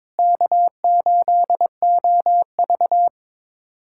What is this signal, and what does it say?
Morse code, heard as a keyed tone, about 22 wpm, 700 Hz. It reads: K8OV
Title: USA Callsign DE Callsign - 4 Characters - 22wpm